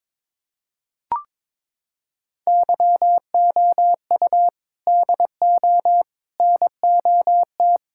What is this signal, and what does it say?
Morse code, heard as a keyed tone, about 22 wpm, 700 Hz. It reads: YOU DO NOT
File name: practice / 22wpm / Sentences with 4-Word Phrases - 22wpm